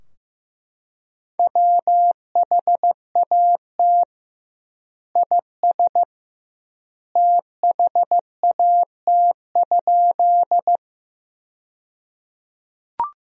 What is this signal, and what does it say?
Morse code, heard as a keyed tone, about 15 wpm, 700 Hz. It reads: WHAT IS THAT?
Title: 2x Word-Spacing - Sentences from Top 100 Words - 15wpm